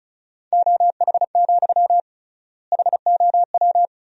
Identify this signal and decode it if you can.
Morse code, heard as a keyed tone, about 34 wpm, 700 Hz. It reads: OH, HOW
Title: 3x Word-Spacing - Sentences from Top 500 Words - 35wpm